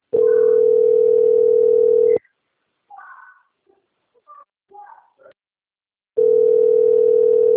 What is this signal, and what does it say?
Ringback tone (North America), 440 + 480 Hz.